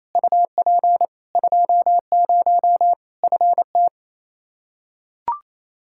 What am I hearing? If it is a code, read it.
Morse code, heard as a keyed tone, about 28 wpm, 700 Hz. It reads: UP 20 FT
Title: K1USN SST QSO Practice 28wpm